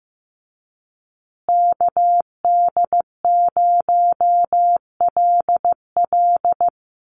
Morse code, heard as a keyed tone, about 15 wpm, 700 Hz. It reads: KD0LL
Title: USA Call Signs Repeated 3x 15wpm